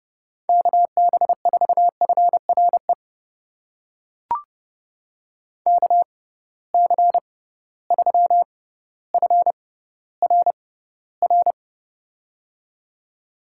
Morse code, heard as a keyed tone, about 30 wpm, spaced as for 15 wpm, 700 Hz. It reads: KB4FRE KC3FRR
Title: USA Call Signs - 6 Characters - 6x Character Spacing - 30wpm